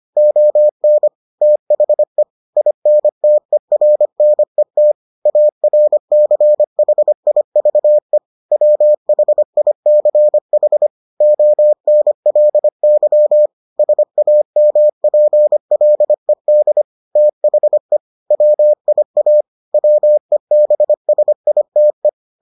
Morse code, heard as a keyed tone, about 25 wpm, 600 Hz. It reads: ON THE INTERNET ARCHIVE WHICH ONLY SAMPLED THE WIA WEBSITE